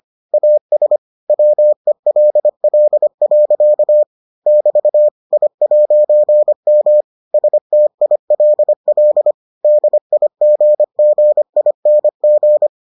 Morse code, heard as a keyed tone, about 25 wpm, 600 Hz. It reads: AS WELL. <BT> I'M STILL DIGGING